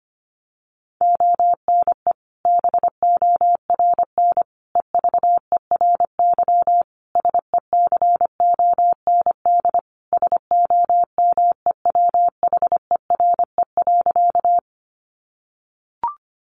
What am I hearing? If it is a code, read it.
Morse code, heard as a keyed tone, about 25 wpm, 700 Hz. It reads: ONE BORN EVERY SECOND SOMEWHERE.
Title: Sentences from Top 1400 Words 25wpm